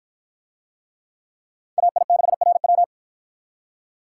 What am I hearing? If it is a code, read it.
Morse code, heard as a keyed tone, about 53 wpm, 700 Hz. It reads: AI6RW